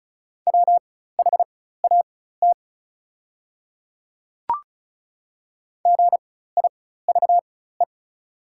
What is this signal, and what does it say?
Morse code, heard as a keyed tone, about 34 wpm, spaced as for 16 wpm, 690 Hz. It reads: WHAT GIVE
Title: 4x Character Spacing - Top 300 Words - 35wpm